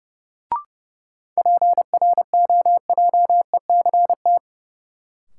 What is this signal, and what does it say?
Morse code, heard as a keyed tone, about 30 wpm, 700 Hz. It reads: PROJECT
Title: Sentences from Top 1400 Words - New Word Drill Alt - 30wpm